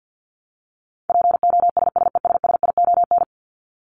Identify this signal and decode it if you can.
Morse code, heard as a keyed tone, about 50 wpm, 700 Hz. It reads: POSSESSION
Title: Top 1251-1500 Words 50wpm